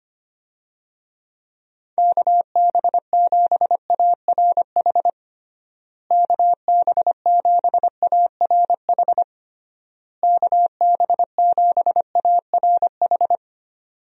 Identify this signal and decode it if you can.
Morse code, heard as a keyed tone, about 25 wpm, 700 Hz. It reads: KB7ARH KB7ARH KB7ARH
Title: USA Call Signs - 6 Characters - Repeated 3x - 25wpm